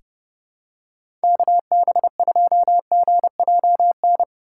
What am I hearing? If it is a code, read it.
Morse code, heard as a keyed tone, about 30 wpm, 700 Hz. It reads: KB2GJN
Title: USA Call Signs - 2x Character Spacing - 30wpm